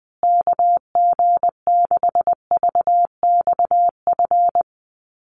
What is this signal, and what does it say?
Morse code, heard as a keyed tone, about 20 wpm, 700 Hz. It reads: KG6VXF
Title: USA Call Signs Repeated 2x 20wpm